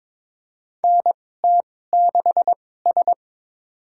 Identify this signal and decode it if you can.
Morse code, heard as a keyed tone, about 22 wpm, spaced as for 16 wpm, 700 Hz. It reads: NT6S
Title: USA Call Signs - 2x Character Spacing - 22wpm